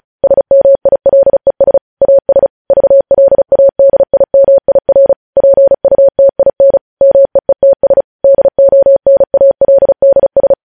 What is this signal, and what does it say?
Morse code, heard as a keyed tone, about 36 wpm, 550 Hz. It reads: SMILES AS VLADIMIR PUTIN MEETS DONALDS
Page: News headlines in Morse code at 35 wpm